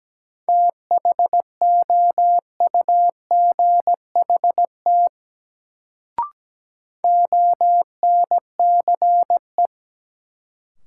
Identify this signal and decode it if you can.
Morse code, heard as a keyed tone, about 17 wpm, 700 Hz. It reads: THOUGHT ONCE
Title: Sentences from Top 300 Words - New Word Drill - 17wpm